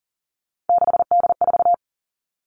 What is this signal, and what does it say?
Morse code, heard as a keyed tone, about 38 wpm, 700 Hz. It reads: 6D4
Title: Three Challenging Letter-Number Combinations 40wpm